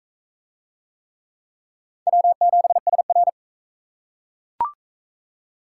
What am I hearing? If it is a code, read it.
Morse code, heard as a keyed tone, about 38 wpm, 700 Hz. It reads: W7SR